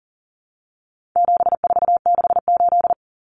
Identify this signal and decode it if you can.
Morse code, heard as a keyed tone, about 40 wpm, 700 Hz. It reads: 7468
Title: Numbers - 4-Digit 40wpm